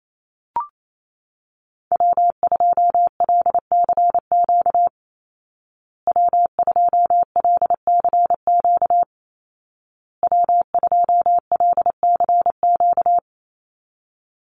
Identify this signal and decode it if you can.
Morse code, heard as a keyed tone, about 28 wpm, 700 Hz. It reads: W2LCQ W2LCQ W2LCQ
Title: Call Signs Repeated 2x - 28wpm